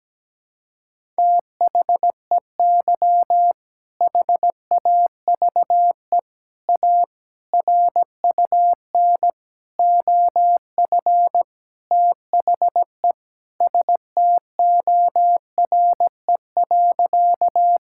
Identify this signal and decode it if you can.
Morse code, heard as a keyed tone, about 17 wpm, 700 Hz. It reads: THEY HAVE A RUN OF THE STORE.
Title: Sentences from Top 1000 Words - Review Part 5 - 17wpm